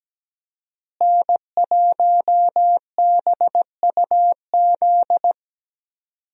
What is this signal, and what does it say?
Morse code, heard as a keyed tone, about 17 wpm, 700 Hz. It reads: N1BUZ
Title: USA Call Signs - 5 Characters - Repeated 2x - 17wpm